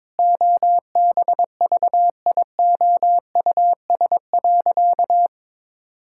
Morse code, heard as a keyed tone, about 22 wpm, 700 Hz. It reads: OBVIOUS.
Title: Sentences from Top 1600 Words 22wpm